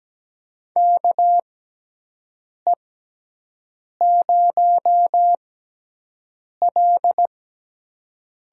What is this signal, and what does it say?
Morse code, heard as a keyed tone, about 17 wpm, spaced as for 6 wpm, 700 Hz. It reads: KE0L
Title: USA Call Signs - 5 Characters - 6x Character Spacing - 17wpm